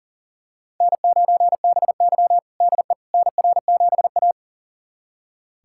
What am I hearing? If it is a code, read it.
Morse code, heard as a keyed tone, about 40 wpm, 700 Hz. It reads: N9BY DE NR7A
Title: USA Callsign DE Callsign - 4 Characters - 40wpm